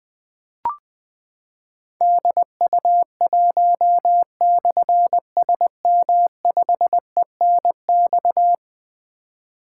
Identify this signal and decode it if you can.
Morse code, heard as a keyed tone, about 20 wpm, 700 Hz. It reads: DU1/SM5ENX
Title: Difficult Call Signs 20wpm